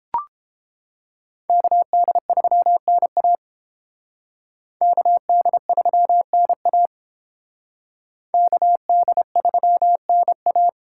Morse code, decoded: KD3NA KD3NA KD3NA

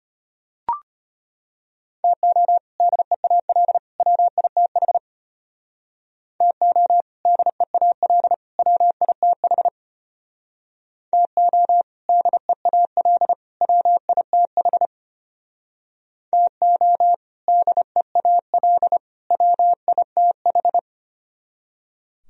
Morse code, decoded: TO DEAL WITH TO DEAL WITH TO DEAL WITH TO DEAL WITH

